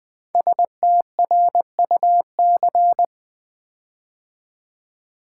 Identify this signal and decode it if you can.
Morse code, heard as a keyed tone, about 20 wpm, 700 Hz. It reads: STRUC
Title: Build A Word - 5 Letters At A Time - 10-Letter Words - 20wpm